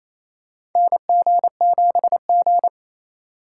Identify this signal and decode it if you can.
Morse code, heard as a keyed tone, about 28 wpm, 700 Hz. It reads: NG7G
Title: USA Call Signs - 2x Character Spacing - 28wpm